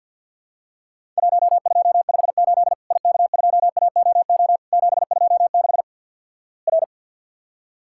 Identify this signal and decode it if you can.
Morse code, heard as a keyed tone, about 48 wpm, 700 Hz. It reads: 1258 IK2AOO 726 R
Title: Worked All Europe DX Contest - QTC Exchange - 50wpm